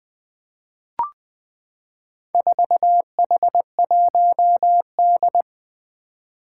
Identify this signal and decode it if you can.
Morse code, heard as a keyed tone, about 21 wpm, 700 Hz. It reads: 4H1D
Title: Four Challenging Letter-Number Combinations 20wpm